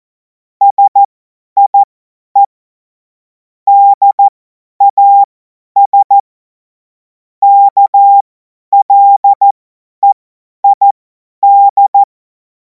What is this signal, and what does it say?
Morse code, heard as a keyed tone, about 14 wpm, spaced as for 10 wpm, 800 Hz. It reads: SIE DAS KLEID
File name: Morsekurs / Witze / Witz 14. April 2026 50 F70